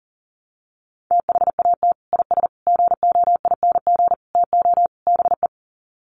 Morse code, heard as a keyed tone, about 40 wpm, 700 Hz. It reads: THAT IS GOING TO BE
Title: Sets of 5 Words Encore2 40wpm